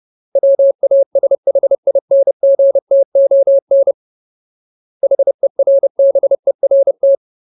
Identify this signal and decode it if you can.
Morse code, heard as a keyed tone, about 30 wpm, 550 Hz. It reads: WASHINGTON HERBERT
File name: Ultimate Word List / 30WPM Ultimate